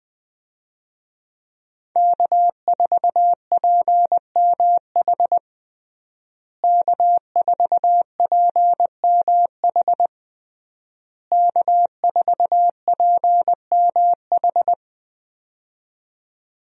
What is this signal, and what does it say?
Morse code, heard as a keyed tone, about 20 wpm, 700 Hz. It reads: K4PMH K4PMH K4PMH
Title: USA Call Signs - 5 Characters - Repeated 3x - 20wpm